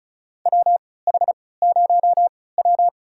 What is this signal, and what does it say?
Morse code, heard as a keyed tone, about 36 wpm, spaced as for 20 wpm, 700 Hz. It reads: WH0W
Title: USA Call Signs - 4 Characters - 3x Character Spacing - 35wpm